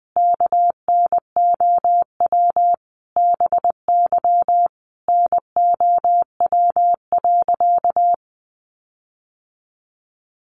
Morse code, heard as a keyed tone, about 20 wpm, 700 Hz. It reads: KNOW BY NOW.
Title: Sentences from Top 1400 Words 20wpm